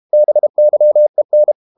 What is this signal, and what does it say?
Morse code, heard as a keyed tone, about 32 wpm, 600 Hz. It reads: BYEN
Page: Ferie 32 wpm